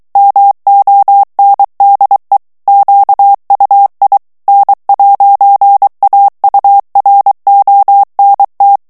Morse code, heard as a keyed tone, about 23 wpm, 800 Hz. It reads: MONDE QUI N'AURONT
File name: CW / FICHIERS-MP3 / FRANCAIS TEXTE CLAIR 2 VITESSE 22 MOTS MINUTES